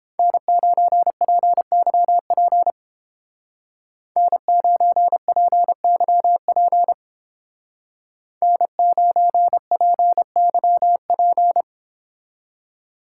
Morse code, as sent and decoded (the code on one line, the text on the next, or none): -. ----. .--. -.-- .--. / -. ----. .--. -.-- .--. / -. ----. .--. -.-- .--.
N9PYP N9PYP N9PYP